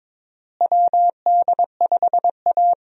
Morse code, heard as a keyed tone, about 22 wpm, 700 Hz. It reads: WD5A
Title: USA Call Signs - 4 Characters - Repeated 3x - 22wpm